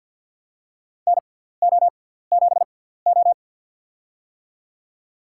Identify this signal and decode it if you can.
Morse code, heard as a keyed tone, about 52 wpm, spaced as for 17 wpm, 700 Hz. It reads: NO7O
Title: USA Call Signs - 4 Characters - 6x Character Spacing - 50wpm